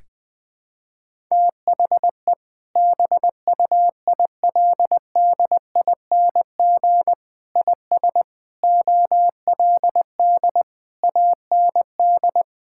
Morse code, heard as a keyed tone, about 20 wpm, 700 Hz. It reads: THE BUILDING IS OLD AND
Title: Sentences from Top 1000 Words - Review Part 5 - 20wpm